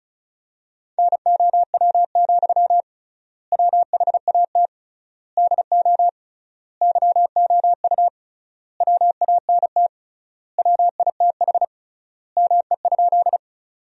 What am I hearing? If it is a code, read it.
Morse code, heard as a keyed tone, about 34 wpm, 700 Hz. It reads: NOW, WHAT DO YOU WANT WITH ME?
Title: 3x Word-Spacing - Sentences from Top 200 Words - 35wpm